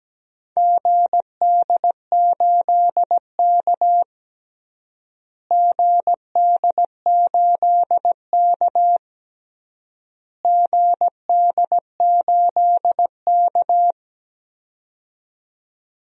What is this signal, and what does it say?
Morse code, heard as a keyed tone, about 17 wpm, 700 Hz. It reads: GD8K GD8K GD8K
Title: DXCC-100 Call Signs - 4 Characters - Repeated 3x - 17wpm